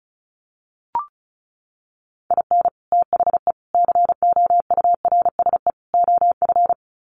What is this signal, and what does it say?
Morse code, heard as a keyed tone, about 34 wpm, 700 Hz. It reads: IN THE COURSE OF
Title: Sets of 4 Words 35wpm